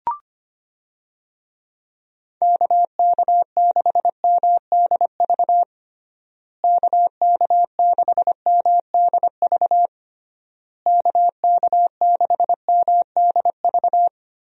Morse code, heard as a keyed tone, about 25 wpm, 700 Hz. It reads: KK6MDV KK6MDV KK6MDV
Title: USA Call Signs - 6 Characters - Repeated 3x - 25wpm